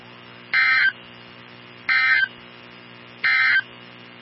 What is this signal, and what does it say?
Emergency Alert System (SAME) tones, heard as harsh buzzing bursts. It is heard from 0.4 to 3.7 s.
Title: NOAA Weather Radio 162.550